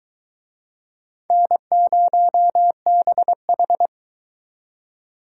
Morse code, heard as a keyed tone, about 21 wpm, 700 Hz. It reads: N0BH